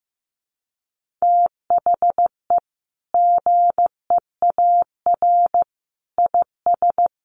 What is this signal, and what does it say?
Morse code, heard as a keyed tone, about 15 wpm, 700 Hz. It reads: THE GEAR IS
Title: Sentences from Top 1400 Words 15wpm